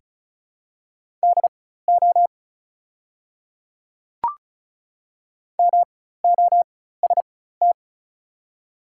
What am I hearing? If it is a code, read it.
Morse code, heard as a keyed tone, about 36 wpm, spaced as for 16 wpm, 700 Hz. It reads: DO MOST
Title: 4x Character Spacing - Top 100 Words - 35wpm